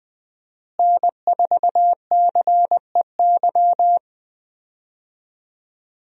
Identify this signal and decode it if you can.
Morse code, heard as a keyed tone, about 20 wpm, 700 Hz. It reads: N4CEY
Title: USA Call Signs - 5 Characters - Repeated 2x - 20wpm